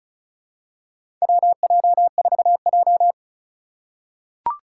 Morse code, heard as a keyed tone, about 34 wpm, 700 Hz. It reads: WJ4J
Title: USA Call Signs - 4 Characters - 6x Character Spacing - 35wpm